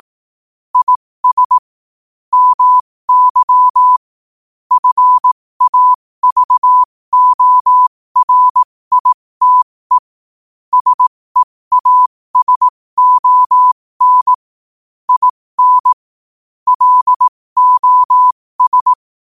Morse code, heard as a keyed tone, about 18 wpm, spaced as for 15 wpm, 1000 Hz. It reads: IS MY FAVORITE SEASON IN LOS
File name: qotdfiles / 15wpm